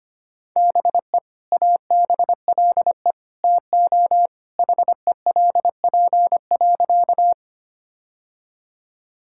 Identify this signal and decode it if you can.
Morse code, heard as a keyed tone, about 25 wpm, 700 Hz. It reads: BE ABLE TO HELP.
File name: practice / 25wpm / Sentences with 2-Word Phrases - 25wpm